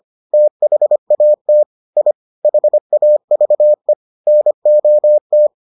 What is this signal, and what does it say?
Morse code, heard as a keyed tone, about 25 wpm, 600 Hz. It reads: THAT I HAVE NOT